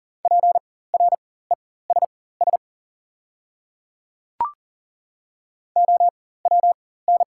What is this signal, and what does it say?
Morse code, heard as a keyed tone, about 41 wpm, spaced as for 19 wpm, 700 Hz. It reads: PRESS OWN
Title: 4x Character Spacing - Top 300 Words - 40wpm